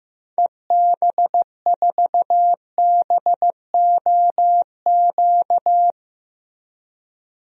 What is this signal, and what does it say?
Morse code, heard as a keyed tone, about 15 wpm, 700 Hz. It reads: EB4BOQ
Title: DXCC-100 Call Signs - 6 Characters - Repeated 3x - 15wpm